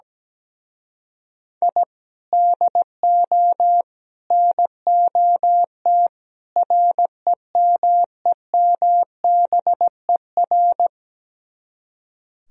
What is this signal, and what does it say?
Morse code, heard as a keyed tone, about 17 wpm, 700 Hz. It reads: I DO NOT REMEMBER
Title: Speed-Racing - Sets of 4 Words 17wpm